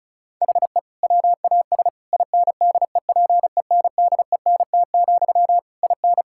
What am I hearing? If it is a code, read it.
Morse code, heard as a keyed tone, about 34 wpm, 700 Hz. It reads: HE WAS INDEPENDENT, IN
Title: Sentences from Top 1000 Words - Review Part 6 - 35wpm